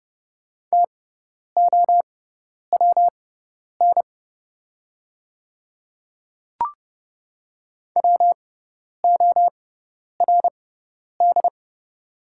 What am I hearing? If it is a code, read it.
Morse code, heard as a keyed tone, about 30 wpm, spaced as for 10 wpm, 700 Hz. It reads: TOWN WORD
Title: 6x Character Spacing - Top 500 Words - 30wpm